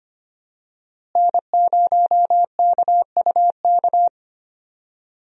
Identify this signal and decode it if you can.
Morse code, heard as a keyed tone, about 25 wpm, 700 Hz. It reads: N0KUK
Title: USA Call Signs - 5 Characters - Repeated 2x - 25wpm